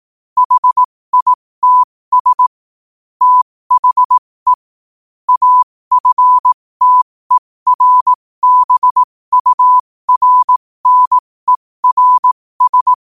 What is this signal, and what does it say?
Morse code, heard as a keyed tone, about 18 wpm, spaced as for 15 wpm, 1000 Hz. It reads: HITS THE AFTERBURNERS
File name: qotdfiles / 15wpm